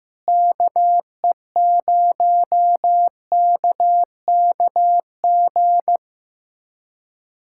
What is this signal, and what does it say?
Morse code, heard as a keyed tone, about 15 wpm, 700 Hz. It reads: KE0KKG
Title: USA Callsign DE Callsign - 6 Characters - 15wpm